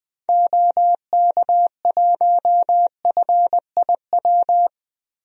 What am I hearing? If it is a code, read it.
Morse code, heard as a keyed tone, about 20 wpm, 700 Hz. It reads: OK1FIW
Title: DXCC-100 Call Signs - 6 Characters - 20wpm